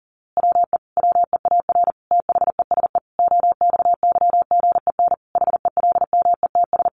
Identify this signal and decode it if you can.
Morse code, heard as a keyed tone, about 40 wpm, 700 Hz. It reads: WE WEAR THESE OXYGEN HELMETS
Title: A Sound Of Thunder 40wpm